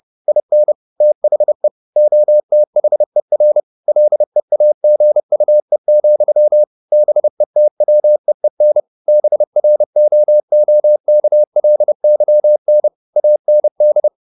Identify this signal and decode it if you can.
Morse code, heard as a keyed tone, about 30 wpm, 600 Hz. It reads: IN THE OTHER LEAGUE, BETWEEN BROOKLYN AND
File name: the old man and the sea / the old man and the sea 1 30wpm